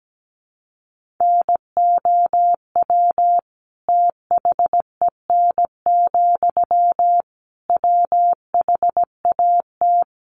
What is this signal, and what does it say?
Morse code, heard as a keyed tone, about 17 wpm, 700 Hz. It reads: NOW THEN, WHAT
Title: Sentences from Top 1100 Words 17wpm